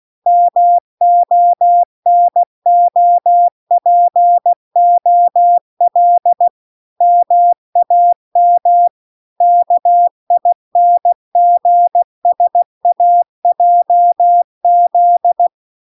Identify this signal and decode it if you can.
Morse code, heard as a keyed tone, about 16 wpm, 700 Hz. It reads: MONOPOL MAM KINGSAJZ